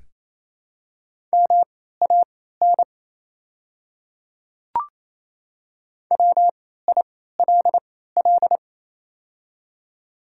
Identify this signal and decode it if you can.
Morse code, heard as a keyed tone, about 27 wpm, spaced as for 16 wpm, 690 Hz. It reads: MAN WILL
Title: 3x Character Spacing - Top 300 Words - 28wpm